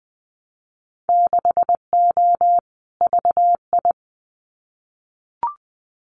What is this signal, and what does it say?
Morse code, heard as a keyed tone, about 20 wpm, 700 Hz. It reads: 6O VI
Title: Winter Field Day Exchanges Encore 20wpm